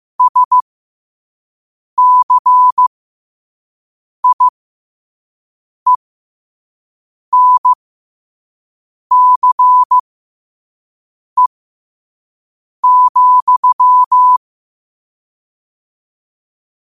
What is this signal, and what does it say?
Morse code, heard as a keyed tone, about 15 wpm, spaced as for 5 wpm, 1000 Hz. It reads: SCIENCE,
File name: qotdfiles / 5wpm